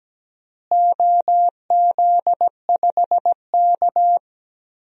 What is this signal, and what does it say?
Morse code, heard as a keyed tone, about 17 wpm, 700 Hz. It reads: OZ5K
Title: DXCC-100 Call Signs - 4 Characters - Repeated 3x - 17wpm